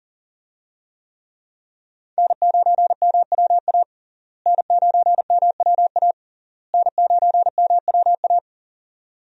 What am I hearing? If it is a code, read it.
Morse code, heard as a keyed tone, about 40 wpm, 700 Hz. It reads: N9MWA N9MWA N9MWA
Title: USA Call Signs - 5 Characters - Repeated 3x - 40wpm